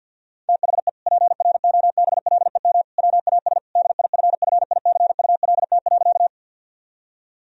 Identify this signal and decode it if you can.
Morse code, heard as a keyed tone, about 51 wpm, 700 Hz. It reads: THE PROBLEM WAS DIFFICULT.
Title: Sentences from Top 1000 Words - Review Part 2 - 50wpm